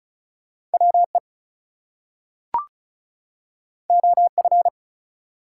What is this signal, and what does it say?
Morse code, heard as a keyed tone, about 36 wpm, 700 Hz. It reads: WE OF
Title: Two Letter Words - 35wpm